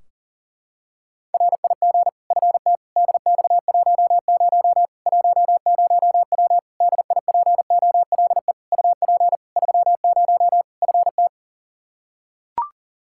Morse code, heard as a keyed tone, about 40 wpm, 700 Hz. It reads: RIG FT DX10 10W DIPOLE UP 30 FT
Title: Practice QSO - Exchange 2 - Info - 40wpm